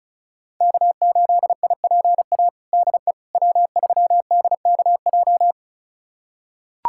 Morse code, heard as a keyed tone, about 36 wpm, 700 Hz. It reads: K8IPA DE W3DKJ
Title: USA Callsign DE Callsign - 5 Characters - 35wpm